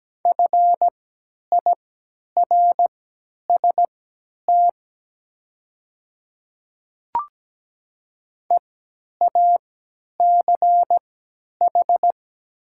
Morse code, heard as a keyed tone, about 17 wpm, spaced as for 10 wpm, 700 Hz. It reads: FIRST EACH
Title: 3x Character Spacing - Top 300 Words - 17wpm